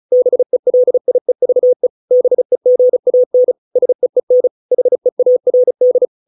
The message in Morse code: BELIEVE BEGAN SEEN HEARD

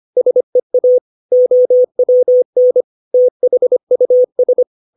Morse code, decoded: SEA OWN THUS

